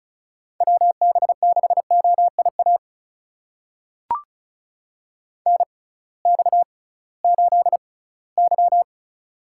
Morse code, heard as a keyed tone, about 36 wpm, spaced as for 21 wpm, 700 Hz. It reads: WB6OIA NX8Y
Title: USA Call Signs - 6x Character Spacing - 35wpm